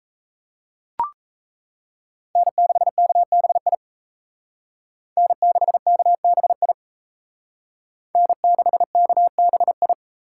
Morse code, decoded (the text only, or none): N6KBI N6KBI N6KBI